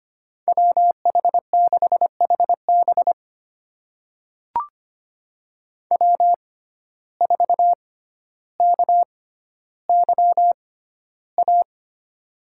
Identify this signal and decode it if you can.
Morse code, heard as a keyed tone, about 25 wpm, spaced as for 13 wpm, 700 Hz. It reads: WH6HB W4KYA
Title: USA Call Signs - 5 Characters - 6x Character Spacing - 25wpm